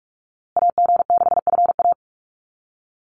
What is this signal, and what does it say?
Morse code, heard as a keyed tone, about 46 wpm, 700 Hz. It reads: AG6FA
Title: USA Call Signs Repeated 3x 45wpm